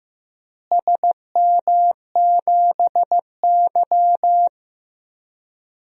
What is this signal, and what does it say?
Morse code, heard as a keyed tone, about 15 wpm, 700 Hz. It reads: SM7Y
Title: DXCC-100 Call Signs - 4 Characters - Repeated 2x - 15wpm